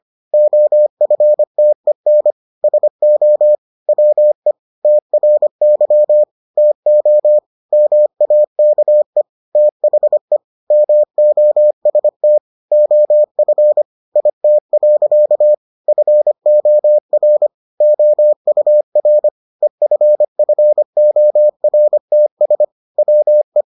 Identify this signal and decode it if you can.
Morse code, heard as a keyed tone, about 25 wpm, 600 Hz. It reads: OFTEN SO WE TRY TO MAKE THE MOST OF IT. FOR OUR EFFORTS WE